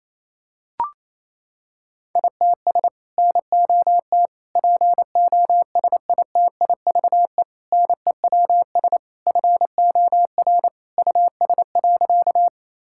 Morse code, heard as a keyed tone, about 28 wpm, 700 Hz. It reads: ITS NOT POSITIVE NEWS FOR US.
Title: Sentences from Top 1000 Words - Review Part 5 - 28wpm